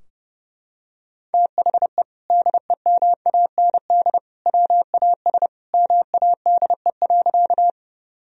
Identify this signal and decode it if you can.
Morse code, heard as a keyed tone, about 30 wpm, 700 Hz. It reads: THE DEMAND WAS MADE.
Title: Sentences from Top 1000 Words - Review Part 4 - 30wpm